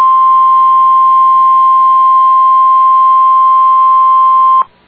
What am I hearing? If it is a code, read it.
Emergency Alert System (SAME) tones (the 1050 Hz attention tone). It is heard from 0.4 to 4.4 s.